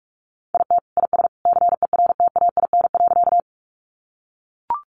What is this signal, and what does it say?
Morse code, heard as a keyed tone, about 44 wpm, 700 Hz. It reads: IT IS CERTAIN.
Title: Sentences from Top 500 Words 45wpm